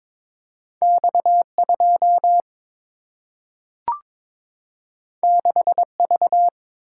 Morse code, heard as a keyed tone, about 22 wpm, 700 Hz. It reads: X 2 6 V
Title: Single-Letter Single-Number 22wpm